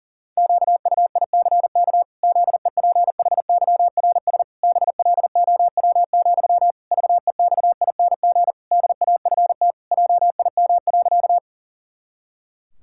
Morse code, heard as a keyed tone, about 40 wpm, 700 Hz. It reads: QUICK ZEPHYRS BLOW, VEXING DAFT JIM.
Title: Pangram Challenge v2 - 40wpm